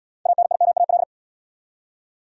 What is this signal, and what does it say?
Morse code, heard as a keyed tone, about 68 wpm, 700 Hz. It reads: ANIMAL